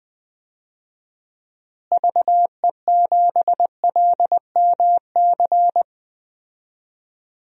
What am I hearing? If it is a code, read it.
Morse code, heard as a keyed tone, about 20 wpm, 700 Hz. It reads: VE7LMC
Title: DXCC-100 Call Signs - 6 Characters - Repeated 3x - 20wpm